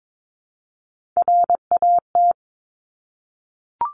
Morse code, heard as a keyed tone, about 22 wpm, 700 Hz. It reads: RAT
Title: Top Three Letter Combinations in Words 22wpm